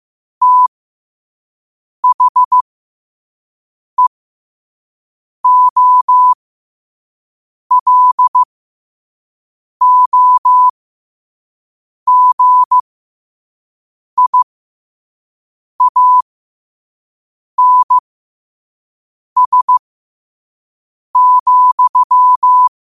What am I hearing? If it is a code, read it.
Morse code, heard as a keyed tone, about 15 wpm, spaced as for 5 wpm, 1000 Hz. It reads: THEOLOGIANS,
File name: qotdfiles / 5wpm